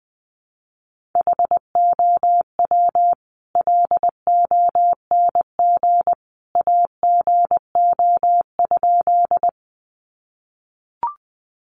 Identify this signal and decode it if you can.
Morse code, heard as a keyed tone, about 20 wpm, 700 Hz. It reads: HOW LONG AGO?
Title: Sentences from Top 500 Words 20wpm